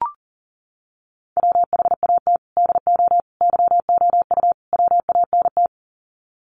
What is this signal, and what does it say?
Morse code, heard as a keyed tone, about 40 wpm, 700 Hz. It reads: WHAT DO YOU WANT
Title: Sets of 4 Words 40wpm